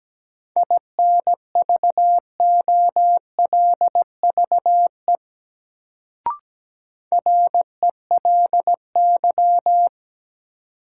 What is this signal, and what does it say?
Morse code, heard as a keyed tone, about 17 wpm, 700 Hz. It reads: INVOLVE RELY
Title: Sentences from Top 1200 Words - New Word Drill - 17wpm